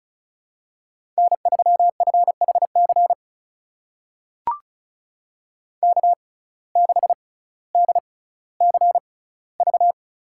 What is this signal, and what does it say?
Morse code, heard as a keyed tone, about 34 wpm, spaced as for 18 wpm, 700 Hz. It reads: N3FHC K6DCV
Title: USA Call Signs - 5 Characters - 6x Character Spacing - 35wpm